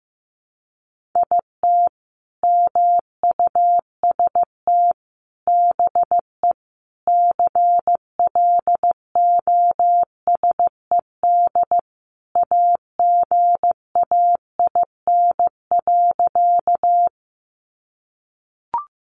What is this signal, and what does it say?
Morse code, heard as a keyed tone, about 15 wpm, 700 Hz. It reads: IT MUST BE CLOSED AGAIN.
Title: Sentences from Top 1400 Words 15wpm